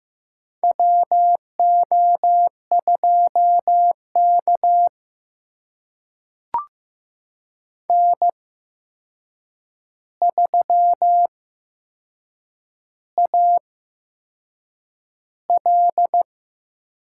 Morse code, heard as a keyed tone, about 15 wpm, spaced as for 6 wpm, 700 Hz. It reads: WO2K N3AL
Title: USA Call Signs - 4 Characters - 8x Character Spacing - 15wpm